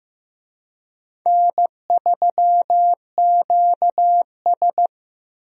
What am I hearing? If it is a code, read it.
Morse code, heard as a keyed tone, about 15 wpm, 700 Hz. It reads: N3QS
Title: USA Call Signs - 4 Characters - Repeated 3x - 15wpm